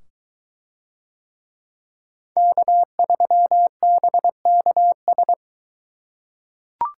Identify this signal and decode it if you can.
Morse code, heard as a keyed tone, about 22 wpm, 700 Hz. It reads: K3BKS